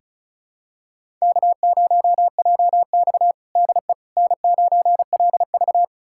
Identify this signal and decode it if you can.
Morse code, heard as a keyed tone, about 36 wpm, 700 Hz. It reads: K0JX DE N9LV
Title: USA Callsign DE Callsign - 4 Characters - 35wpm